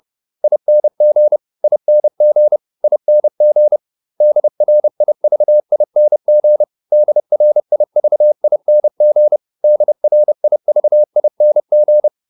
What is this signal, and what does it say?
Morse code, heard as a keyed tone, about 30 wpm, 600 Hz. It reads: ING ING ING DRIVING DRIVING DRIVING